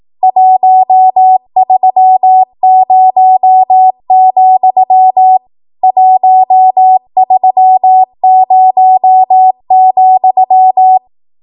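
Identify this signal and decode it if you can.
Morse code, heard as a keyed tone, about 18 wpm, 750 Hz. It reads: 130, 130,
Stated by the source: ARRL 18 wpm Morse code practice archive transmission.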